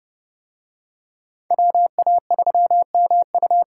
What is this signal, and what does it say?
Morse code, heard as a keyed tone, about 28 wpm, 700 Hz. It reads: WA3MU